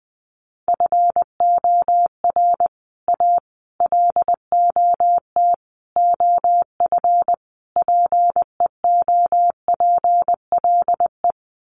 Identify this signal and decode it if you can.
Morse code, heard as a keyed tone, about 20 wpm, 700 Hz. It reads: FOR A LOT OF PEOPLE
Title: Sets of 5 Words Encore 20wpm